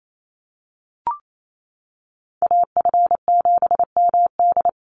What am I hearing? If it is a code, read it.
Morse code, heard as a keyed tone, about 28 wpm, 700 Hz. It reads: AF7MD
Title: NA SOTA Chaser Callsigns 28wpm